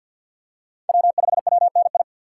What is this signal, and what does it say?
Morse code, heard as a keyed tone, about 48 wpm, 700 Hz. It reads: W5WNI
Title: USA Call Signs - 5 Characters - 3x Character Spacing - 50wpm